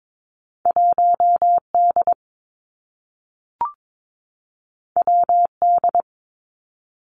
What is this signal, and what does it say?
Morse code, heard as a keyed tone, about 22 wpm, 700 Hz. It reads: 1 D W D
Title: Two Challenging Letter-Number Combinations 22wpm